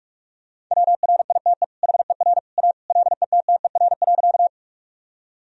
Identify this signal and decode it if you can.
Morse code, heard as a keyed tone, about 44 wpm, 700 Hz. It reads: WRITE HER A LETTER.
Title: Sentences from Top 1000 Words - Review Part 1 - 45wpm